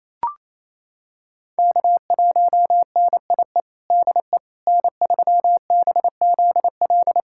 Morse code, heard as a keyed tone, about 28 wpm, 700 Hz. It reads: K1NIE DE N3BZL
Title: USA Callsign DE Callsign - 5 Characters - 28wpm